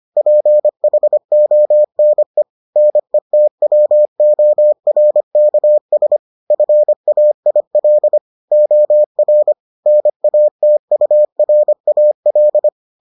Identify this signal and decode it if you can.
Morse code, heard as a keyed tone, about 25 wpm, 600 Hz. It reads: PHONE NETWORKS FAIL OR NATURAL